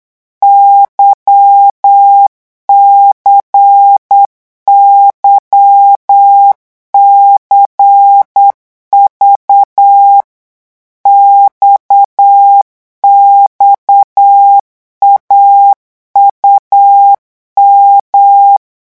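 Morse code, heard as a keyed tone, about 8 wpm, 800 Hz. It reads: YCYCV XXAUM
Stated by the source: Morse code revision lesson at 8 words per minute, from a French amateur radio course.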